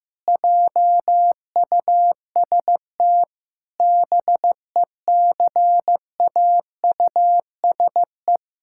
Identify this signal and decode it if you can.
Morse code, heard as a keyed tone, about 15 wpm, 700 Hz. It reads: JUST BECAUSE
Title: Pangram Challenge v2 - 15wpm